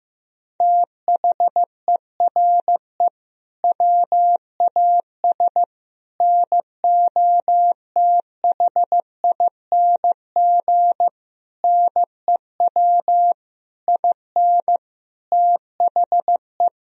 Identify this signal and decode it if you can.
Morse code, heard as a keyed tone, about 15 wpm, 700 Hz. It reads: THERE WAS NOTHING NEW IN THE
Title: Sentences from Top 1000 Words - Review Part 3 - 15wpm